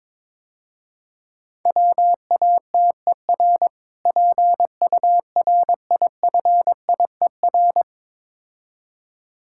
Morse code, heard as a keyed tone, about 22 wpm, 700 Hz. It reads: WATER PURIFIER
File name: Easy Words - Speed-Racing with Hints - 22wpm